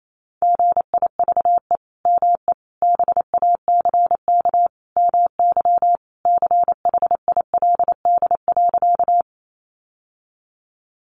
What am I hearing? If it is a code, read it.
Morse code, heard as a keyed tone, about 28 wpm, 700 Hz. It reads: GIVE ME BACK MY CHILD.
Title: Sentences from Top 800 Words 28wpm